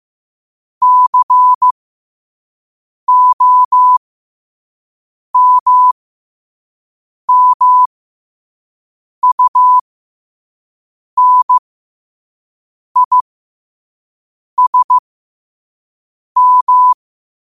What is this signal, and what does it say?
Morse code, heard as a keyed tone, about 15 wpm, spaced as for 5 wpm, 1000 Hz. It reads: COMMUNISM
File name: qotdfiles / 5wpm